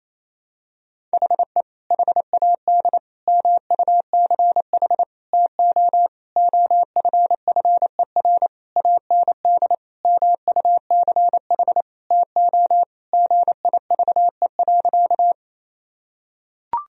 Morse code, heard as a keyed tone, about 28 wpm, 700 Hz. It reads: HE HAD MUCH TO OFFER AND MUCH TO GIVE.
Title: Sentences from Top 1000 Words - Review Part 4 - 28wpm